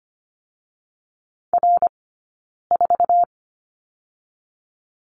Morse code, heard as a keyed tone, about 25 wpm, 700 Hz. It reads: R 4
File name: Lesson 17d - Letter u - Two Characters 25wpm